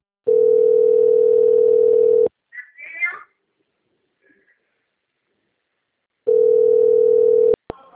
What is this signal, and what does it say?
Ringback tone (North America), 440 + 480 Hz.